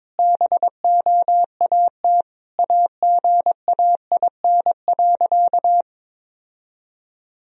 Morse code, heard as a keyed tone, about 22 wpm, 700 Hz. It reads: BOAT AGAIN.
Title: Sentences from Top 1000 Words - Review Part 3 - 22wpm